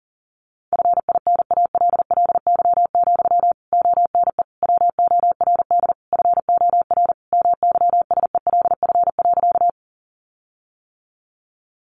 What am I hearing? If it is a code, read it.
Morse code, heard as a keyed tone, about 40 wpm, 700 Hz. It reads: FINALLY, ONE WORD FOR MYSELF.
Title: Sentences from Top 1300 Words 40wpm